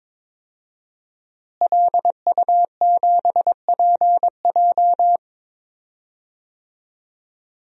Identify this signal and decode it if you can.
Morse code, heard as a keyed tone, about 22 wpm, 700 Hz. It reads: LU7PJ
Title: DXCC-100 Call Signs - 5 Characters - Repeated 3x - 22wpm